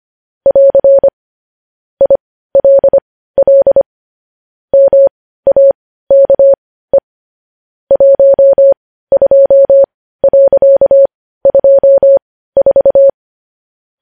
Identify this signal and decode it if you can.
Morse code, heard as a keyed tone, about 25 wpm, spaced as for 15 wpm, 550 Hz. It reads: <AR> ILL MAKE 12.24
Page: news-headlines-in-morse-code-at-15-wpm